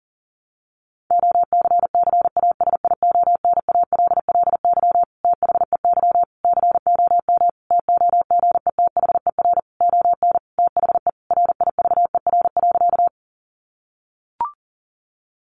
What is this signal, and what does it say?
Morse code, heard as a keyed tone, about 40 wpm, 700 Hz. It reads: OCCASIONALLY THEY COM TOGETHER ON THE RIVER.
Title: Sentences from Top 1300 Words 40wpm